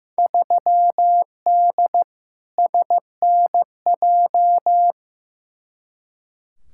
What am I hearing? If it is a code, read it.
Morse code, heard as a keyed tone, about 15 wpm, 700 Hz. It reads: 3D SNJ
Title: Field Day Exchanges Encore - 15wpm